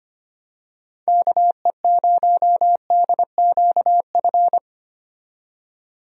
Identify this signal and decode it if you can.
Morse code, heard as a keyed tone, about 25 wpm, 700 Hz. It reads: KE0DQF
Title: USA Call Signs - 6 Characters - 25wpm